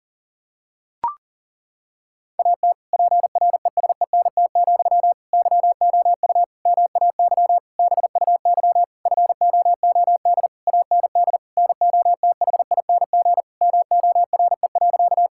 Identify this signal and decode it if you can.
Morse code, heard as a keyed tone, about 40 wpm, 700 Hz. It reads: AT PRESENT, YOU MAY BUY FOOD AND NOTHING MORE.
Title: Sentences from Top 1000 Words - Review Part 5 - 40wpm